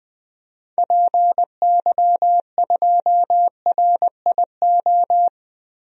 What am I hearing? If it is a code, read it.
Morse code, heard as a keyed tone, about 20 wpm, 700 Hz. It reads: PY2RIO
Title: DXCC-100 Call Signs - 6 Characters - Repeated 3x - 20wpm